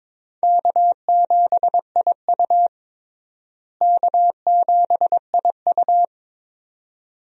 Morse code, heard as a keyed tone, about 22 wpm, 700 Hz. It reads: K7IU K7IU
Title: USA Call Signs - 4 Characters - Repeated 2x - 22wpm